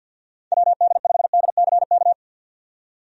Morse code, heard as a keyed tone, about 52 wpm, 700 Hz. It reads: WD5DCK
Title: USA Call Signs - 6 Characters - 50wpm